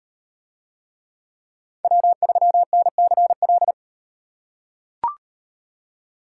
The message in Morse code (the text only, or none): W3NCL